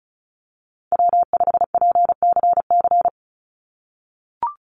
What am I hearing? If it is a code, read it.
Morse code, heard as a keyed tone, about 34 wpm, 700 Hz. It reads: W5PCC
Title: USA Call Signs 35wpm